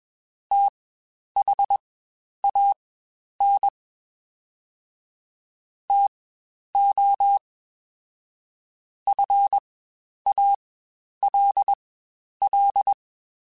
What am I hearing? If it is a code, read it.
Morse code, heard as a keyed tone, about 21 wpm, spaced as for 10 wpm, 800 Hz. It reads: THAN TO FALL
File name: Quotes Spaced / Quotes Spaced-mp3s / StandFor sp @20